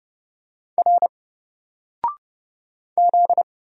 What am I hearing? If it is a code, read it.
Morse code, heard as a keyed tone, about 30 wpm, 700 Hz. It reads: R Z
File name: Single-Letter-Number-30wpm